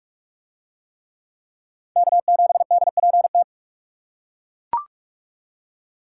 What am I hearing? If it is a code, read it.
Morse code, heard as a keyed tone, about 41 wpm, 700 Hz. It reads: K7DPT